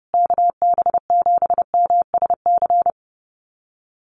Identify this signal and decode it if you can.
Morse code, heard as a keyed tone, about 30 wpm, 700 Hz. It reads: KB7MSC
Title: USA Call Signs Repeated 2x 30wpm